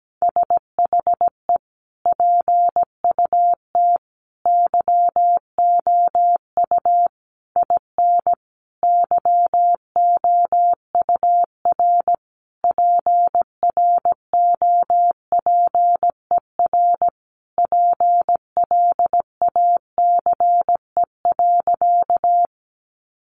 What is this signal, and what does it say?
Morse code, heard as a keyed tone, about 17 wpm, 700 Hz. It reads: SHE PUT YOU IN YOUR PROPER PLACE.
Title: Sentences from Top 800 Words 17wpm